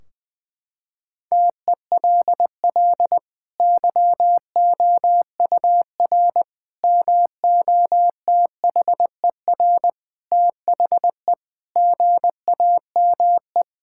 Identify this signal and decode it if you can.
Morse code, heard as a keyed tone, about 20 wpm, 700 Hz. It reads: TELL YOUR MOTHER THE GAME
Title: Sentences from Top 1600 Words 20wpm